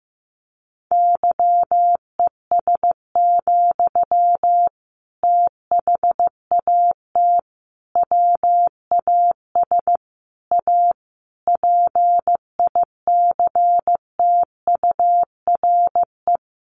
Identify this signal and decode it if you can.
Morse code, heard as a keyed tone, about 15 wpm, 700 Hz. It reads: YES, THAT WAS A PICTURE
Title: Sentences from Top 500 Words 15wpm